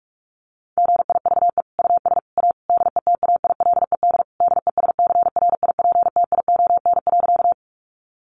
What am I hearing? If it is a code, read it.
Morse code, heard as a keyed tone, about 44 wpm, 700 Hz. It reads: GIVE US A DETAILED DESCRIPTION.
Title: Sentences from Top 1500 Words 45wpm